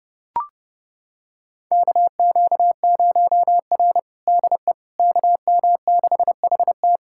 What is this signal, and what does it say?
Morse code, heard as a keyed tone, about 30 wpm, 700 Hz. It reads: KQ0R DE KM6HT
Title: USA Callsign DE Callsign 30wpm